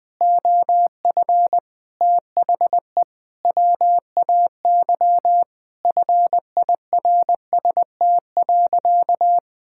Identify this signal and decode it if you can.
Morse code, heard as a keyed tone, about 20 wpm, 700 Hz. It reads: OF THE WAY FIRST.
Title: Sentences with 7-Word Phrases - 20wpm